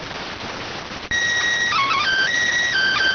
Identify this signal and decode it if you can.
SSTV (slow-scan television), PD 290.